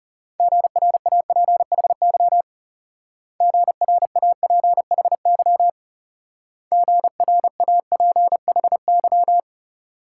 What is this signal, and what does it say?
Morse code, heard as a keyed tone, about 38 wpm, 700 Hz. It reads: GRAPHY GRAPHY GRAPHY
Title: Speed-Racing Multi-letter Phonemes - 30wpm